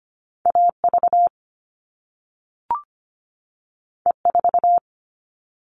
Morse code, heard as a keyed tone, about 25 wpm, 700 Hz. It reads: A V E 4
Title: All Cut Numbers - 2-Digit 25wpm